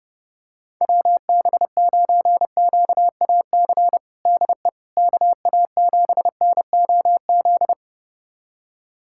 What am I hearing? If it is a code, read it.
Morse code, heard as a keyed tone, about 30 wpm, 700 Hz. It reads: WB9QAC DE KA7NOZ
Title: USA Callsign DE Callsign - 6 Characters - 30wpm